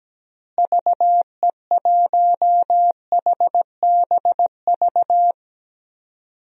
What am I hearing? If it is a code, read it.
Morse code, heard as a keyed tone, about 17 wpm, 700 Hz. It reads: VE1HBV
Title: DXCC-100 Call Signs - 6 Characters - Repeated 3x - 17wpm